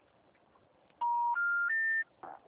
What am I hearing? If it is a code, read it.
Special information tone.